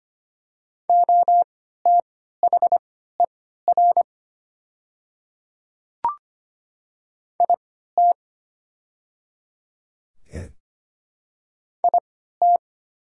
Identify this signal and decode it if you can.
Morse code, heard as a keyed tone, about 25 wpm, spaced as for 14 wpm, 700 Hz. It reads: OTHER IT IT
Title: 3x Character Spacing - Top 100 Words - 25wpm